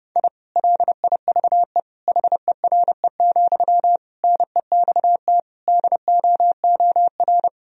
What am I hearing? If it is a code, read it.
Morse code, heard as a keyed tone, about 30 wpm, 700 Hz. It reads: I LIVE HERE, NEXT DOOR
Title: Sentences from Top 1000 Words - Review Part 1 - 30wpm